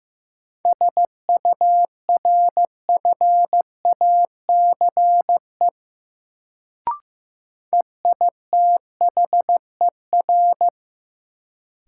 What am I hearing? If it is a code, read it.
Morse code, heard as a keyed tone, about 15 wpm, 700 Hz. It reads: SURFACE EITHER
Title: Sentences from Top 600 Words - New Word Drill - 15wpm